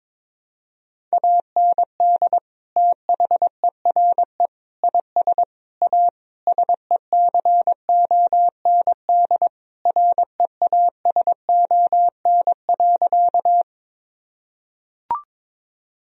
Morse code, heard as a keyed tone, about 22 wpm, 700 Hz. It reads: AND THERE IS A SECOND REASON.
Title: Sentences from Top 1000 Words - Review Part 1 - 22wpm